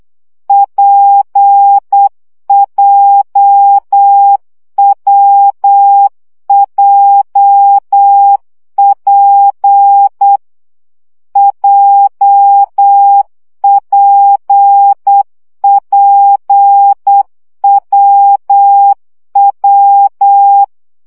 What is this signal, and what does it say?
Morse code, heard as a keyed tone, about 8 wpm, 800 Hz. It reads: PJWJP JPPWW
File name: CW gyakorlas w p j 7 baud